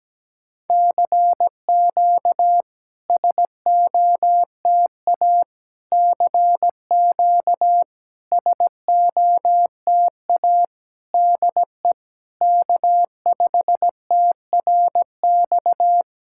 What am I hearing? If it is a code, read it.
Morse code, heard as a keyed tone, about 17 wpm, 700 Hz. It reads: CQ SOTA CQ SOTA DE K5TRX
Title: SOTA QSO Practice 17wpm